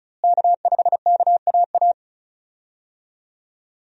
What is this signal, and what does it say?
Morse code, heard as a keyed tone, about 34 wpm, 700 Hz. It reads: K5KAA
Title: USA Call Signs - 5 Characters - Repeated 2x - 35wpm